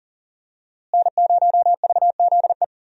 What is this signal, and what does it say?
Morse code, heard as a keyed tone, about 39 wpm, 700 Hz. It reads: N0VZE
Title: USA Call Signs - 5 Characters - Repeated 3x - 40wpm